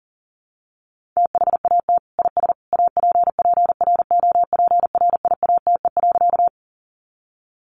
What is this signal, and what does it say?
Morse code, heard as a keyed tone, about 40 wpm, 700 Hz. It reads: THAT IS APPROPRIATE.
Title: Sentences from Top 900 Words 40wpm